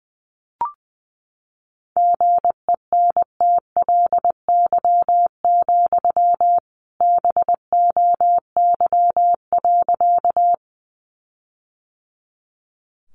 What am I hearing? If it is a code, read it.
Morse code, heard as a keyed tone, about 19 wpm, 700 Hz. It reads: GENTLY, BOY.
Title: Sentences from Top 1500 Words 20wpm